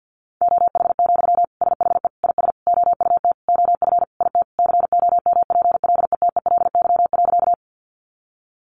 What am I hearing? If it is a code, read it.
Morse code, heard as a keyed tone, about 50 wpm, 700 Hz. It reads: OH, SHE IS OUT OF IT COMPLETELY.
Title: Sentences from Top 1100 Words 50wpm